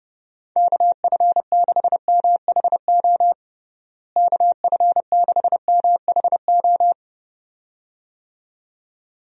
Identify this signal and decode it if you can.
Morse code, heard as a keyed tone, about 32 wpm, 700 Hz. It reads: KF6MHO KF6MHO
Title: USA Call Signs - 6 Characters - Repeated 2x - 30wpm